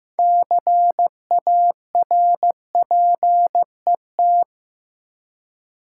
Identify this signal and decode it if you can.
Morse code, heard as a keyed tone, about 15 wpm, 700 Hz. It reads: CARPET
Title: Six Letter Words - 15wpm